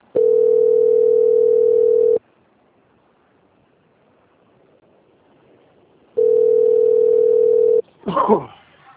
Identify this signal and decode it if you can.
Ringback tone (North America), 440 + 480 Hz.